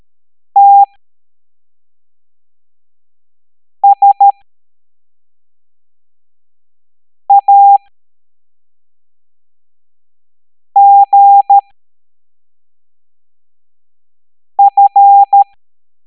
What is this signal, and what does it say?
Morse code, heard as a keyed tone, about 13 wpm, spaced as for 3 wpm, 800 Hz. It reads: TSAGF